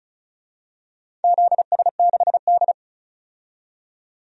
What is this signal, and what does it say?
Morse code, heard as a keyed tone, about 34 wpm, 700 Hz. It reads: ZS6D
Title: DXCC-100 Call Signs - 4 Characters - 35wpm